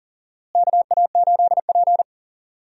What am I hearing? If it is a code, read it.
Morse code, heard as a keyed tone, about 40 wpm, 700 Hz. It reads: KA8P
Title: USA Call Signs - 4 Characters - Repeated 2x - 40wpm